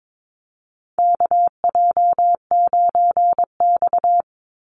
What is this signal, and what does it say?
Morse code, heard as a keyed tone, about 22 wpm, 700 Hz. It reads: KJ9X
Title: NA SOTA Chaser Callsigns 22wpm